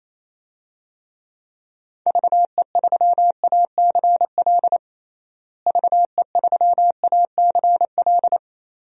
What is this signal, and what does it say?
Morse code, heard as a keyed tone, about 28 wpm, 700 Hz. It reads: VE3ACL VE3ACL
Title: DXCC-100 Call Signs - 6 Characters - Repeated 3x - 28wpm